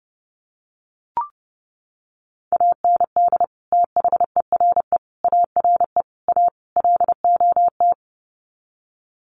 Morse code, heard as a keyed tone, about 30 wpm, 700 Hz. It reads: AND THERE ARE A LOT
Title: Sets of 5 Words Encore2 30wpm